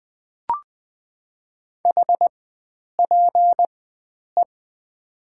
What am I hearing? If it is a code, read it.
Morse code, heard as a keyed tone, about 20 wpm, spaced as for 9 wpm, 700 Hz. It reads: HPE
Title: Top 90 QSO Elements - 4x Character Spacing - 20wpm